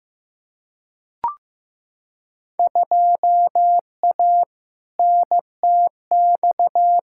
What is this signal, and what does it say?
Morse code, heard as a keyed tone, about 15 wpm, 700 Hz. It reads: 2A NTX
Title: Field Day Exchanges - 15wpm